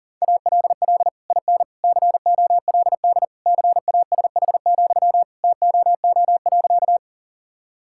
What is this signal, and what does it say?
Morse code, heard as a keyed tone, about 40 wpm, 700 Hz. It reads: ALL IN COLD CASH, TOO.
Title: Sentences from Top 1000 Words - Review Part 5 - 40wpm